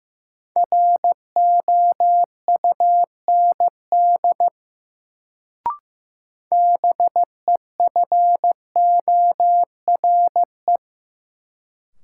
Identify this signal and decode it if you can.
Morse code, heard as a keyed tone, about 15 wpm, 700 Hz. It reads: ROUND BEFORE
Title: Sentences from Top 200 Words - New Word Drill - 15wpm